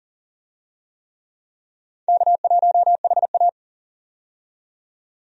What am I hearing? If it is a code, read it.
Morse code, heard as a keyed tone, about 40 wpm, 700 Hz. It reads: K1HA
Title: USA Call Signs - 4 Characters - 40wpm